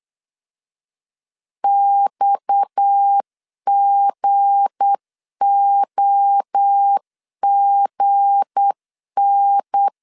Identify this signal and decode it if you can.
Morse code, heard as a keyed tone, about 8 wpm, 800 Hz. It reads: XGOGN
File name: LEZIONI / 7 GF9Z / 7 G 8wpm 3118